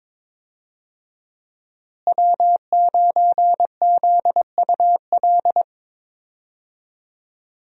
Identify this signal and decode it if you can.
Morse code, heard as a keyed tone, about 22 wpm, 700 Hz. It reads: W9ZUL
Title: USA Call Signs - 5 Characters - Repeated 3x - 22wpm